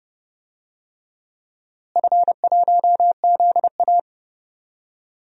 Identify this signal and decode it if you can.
Morse code, heard as a keyed tone, about 30 wpm, 700 Hz. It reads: F1ZA
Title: DXCC-100 Call Signs - 4 Characters - Repeated 3x - 30wpm